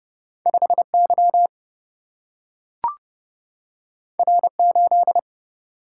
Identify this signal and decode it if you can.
Morse code, heard as a keyed tone, about 31 wpm, 700 Hz. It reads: 5Y R8
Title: Single-Letter Single-Number 30wpm